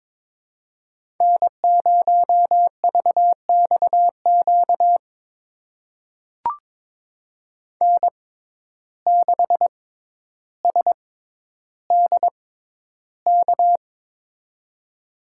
Morse code, heard as a keyed tone, about 22 wpm, spaced as for 11 wpm, 700 Hz. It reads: N0VXQ N6SDK
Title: USA Call Signs - 6x Character Spacing - 22wpm